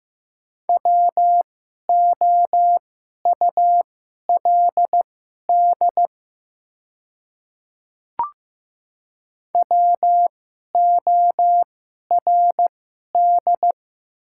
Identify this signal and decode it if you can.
Morse code, heard as a keyed tone, about 15 wpm, spaced as for 11 wpm, 700 Hz. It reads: WOULD WORD
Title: 2x Character Spacing - Top 100 Words - 15wpm